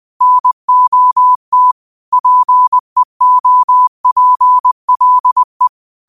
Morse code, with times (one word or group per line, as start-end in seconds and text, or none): 0.2-1.7 NOT
2.1-5.7 PEOPLE